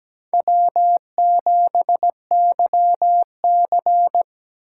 Morse code, heard as a keyed tone, about 17 wpm, 700 Hz. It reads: W7YC
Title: USA Call Signs - 4 Characters - 4x Character Spacing - 17wpm